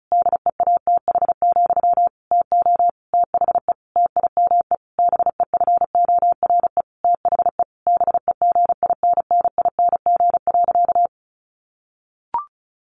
Morse code, heard as a keyed tone, about 34 wpm, 700 Hz. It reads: DEATH, TO THE TIME BEFORE THE BEGINNING.
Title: A Sound Of Thunder 35wpm